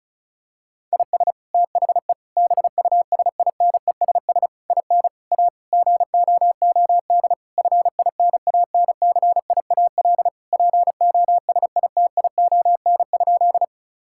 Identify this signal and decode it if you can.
Morse code, heard as a keyed tone, about 34 wpm, 700 Hz. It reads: IS THE BUSINESS IN A GOOD FINANCIAL POSITION?
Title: Sentences from Top 1000 Words - Review Part 5 - 35wpm